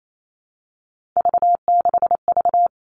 Morse code, heard as a keyed tone, about 28 wpm, 700 Hz. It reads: V6V
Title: All Cut Numbers - 3-Digit 28wpm